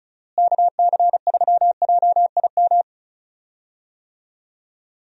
Morse code, heard as a keyed tone, about 36 wpm, 700 Hz. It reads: KC3JIM
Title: USA Call Signs - 6 Characters - Repeated 3x - 35wpm